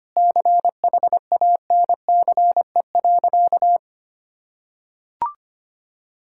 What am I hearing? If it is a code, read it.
Morse code, heard as a keyed tone, about 25 wpm, 700 Hz. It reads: CHANCE.
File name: practice / 25wpm / Sentences with 3-Word Phrases - 25wpm